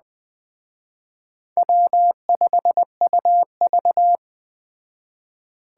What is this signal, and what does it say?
Morse code, heard as a keyed tone, about 20 wpm, 700 Hz. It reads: W5UV
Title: Speed Racing - USA 1x2 Call Signs 17wpm